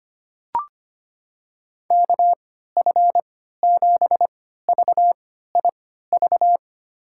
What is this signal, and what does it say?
Morse code, heard as a keyed tone, about 25 wpm, spaced as for 14 wpm, 700 Hz. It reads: KF7VIV
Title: USA Call Signs - 3x Character Spacing - 25wpm